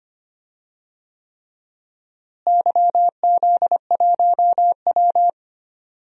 Morse code, heard as a keyed tone, about 25 wpm, 700 Hz. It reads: YZ1W
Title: DXCC-100 Call Signs - 4 Characters - Repeated 3x - 25wpm